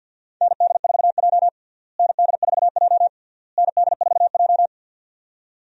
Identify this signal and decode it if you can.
Morse code, heard as a keyed tone, about 52 wpm, 700 Hz. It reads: ND4J ND4J ND4J
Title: USA Call Signs - 4 Characters - Repeated 3x - 50wpm